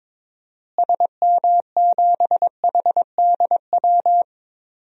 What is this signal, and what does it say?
Morse code, heard as a keyed tone, about 22 wpm, 700 Hz. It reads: SM7HDW
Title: DXCC-100 Call Signs - 6 Characters - Repeated 2x - 22wpm